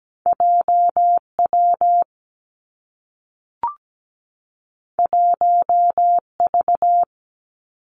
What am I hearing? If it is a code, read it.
Morse code, heard as a keyed tone, about 17 wpm, 700 Hz. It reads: JW 1V
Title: Two Challenging Letter-Number Combinations 17wpm